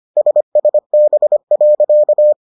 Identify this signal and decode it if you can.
Morse code, heard as a keyed tone, about 25 wpm, 600 Hz. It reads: SSB.